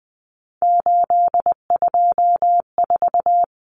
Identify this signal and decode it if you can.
Morse code, heard as a keyed tone, about 20 wpm, 700 Hz. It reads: 824
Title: Numbers - 3-Digit 20wpm